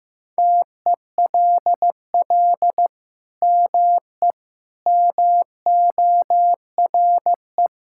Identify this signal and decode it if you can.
Morse code, heard as a keyed tone, about 16 wpm, 700 Hz. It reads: TELL ME MORE
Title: Sentences from Top 1000 Words - Review Part 5 - 15wpm